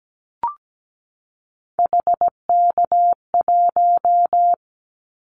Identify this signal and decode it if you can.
Morse code, heard as a keyed tone, about 17 wpm, 700 Hz. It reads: HK1
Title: Callsign Prefixes for Top 100 DXCC - 17wpm